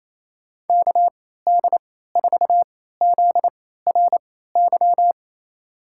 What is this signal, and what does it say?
Morse code, heard as a keyed tone, about 28 wpm, spaced as for 16 wpm, 700 Hz. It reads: KD4ZRY
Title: USA Call Signs - 6 Characters - 3x Character Spacing - 28wpm